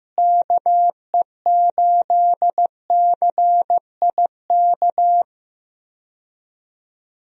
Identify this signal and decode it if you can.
Morse code, heard as a keyed tone, about 15 wpm, 700 Hz. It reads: KE8CIK
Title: USA Call Signs - 6 Characters - Repeated 3x - 15wpm